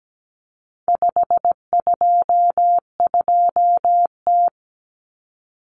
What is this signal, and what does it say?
Morse code, heard as a keyed tone, about 17 wpm, 700 Hz. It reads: 522T
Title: Common Cut Numbers - 4-Digit 17wpm